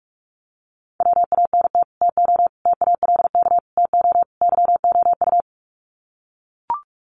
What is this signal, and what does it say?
Morse code, heard as a keyed tone, about 44 wpm, 700 Hz. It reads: WANT TO TALK TO YOU
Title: Sets of 5 Words Encore 45wpm